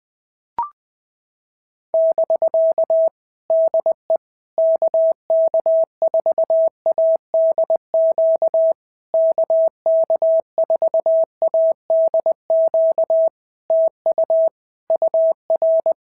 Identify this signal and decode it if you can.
Morse code, heard as a keyed tone, about 20 wpm, 650 Hz. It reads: <BK> DE KK4ADQ KK4ADQ TU UR
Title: POTA QSO Practice - 20wpm